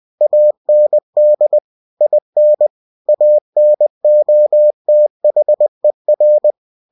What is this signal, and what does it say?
Morse code, heard as a keyed tone, about 20 wpm, 600 Hz. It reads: AND IN ANOTHER
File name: cw-books / winnie-the-pooh / winnie-20wpm-page-0072